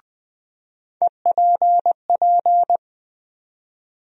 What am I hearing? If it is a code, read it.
Morse code, heard as a keyed tone, about 20 wpm, 700 Hz. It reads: EPP
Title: USA Call Sign Postfixes - 3 Letters 20wpm